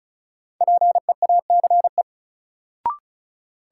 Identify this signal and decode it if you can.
Morse code, heard as a keyed tone, about 36 wpm, 700 Hz. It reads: PEACE
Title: Sentences from Top 1500 Words - New Word Drill - 35wpm